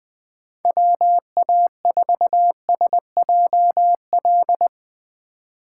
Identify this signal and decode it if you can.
Morse code, heard as a keyed tone, about 20 wpm, 700 Hz. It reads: WA4SJL